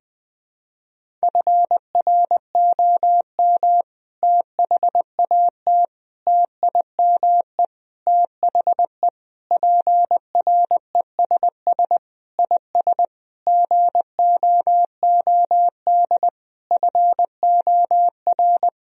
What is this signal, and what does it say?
Morse code, heard as a keyed tone, about 19 wpm, 700 Hz. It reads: FROM THAT TIME THE PRESS IS GOOD FOR
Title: Sentences from Top 1000 Words - Review Part 1 - 20wpm